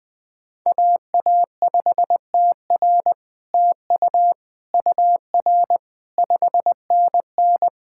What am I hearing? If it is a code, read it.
Morse code, heard as a keyed tone, about 20 wpm, 700 Hz. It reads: AA5TR TU UR 5NN
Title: SOTA QSO Practice 20wpm